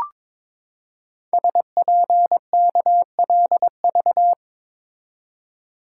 Morse code, heard as a keyed tone, about 23 wpm, 700 Hz. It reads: SPKLV
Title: Code Groups - 5 Letters - 22wpm